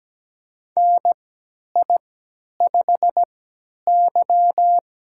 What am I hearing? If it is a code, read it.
Morse code, heard as a keyed tone, about 17 wpm, spaced as for 10 wpm, 700 Hz. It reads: NI5Y
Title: USA Call Signs - 4 Characters - 3x Character Spacing - 17wpm